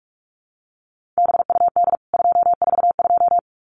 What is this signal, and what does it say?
Morse code, heard as a keyed tone, about 44 wpm, 700 Hz. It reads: BUD 242
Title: CWT Exchange 45wpm